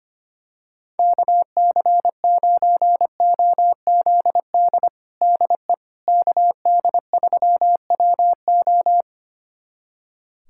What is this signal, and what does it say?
Morse code, heard as a keyed tone, about 25 wpm, 700 Hz. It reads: KC9OZD DE KD3WO
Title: USA Callsign DE Callsign 25wpm